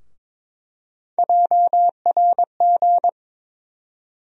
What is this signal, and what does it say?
Morse code, heard as a keyed tone, about 22 wpm, 700 Hz. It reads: JRG
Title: USA Call Sign Postfixes - 3 Letters 22wpm